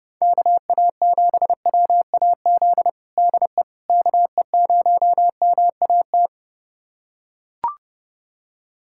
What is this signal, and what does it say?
Morse code, heard as a keyed tone, about 32 wpm, 700 Hz. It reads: KA7WAZ DE KE0MAT
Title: USA Callsign DE Callsign - 6 Characters - 30wpm